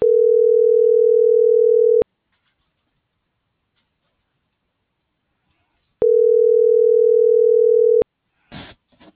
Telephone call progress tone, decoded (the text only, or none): ringback tone